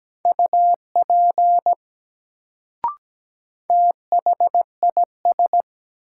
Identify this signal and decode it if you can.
Morse code, heard as a keyed tone, about 17 wpm, 700 Hz. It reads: UP THIS
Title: Sentences from Top 100 Words - New Word Drill - 17wpm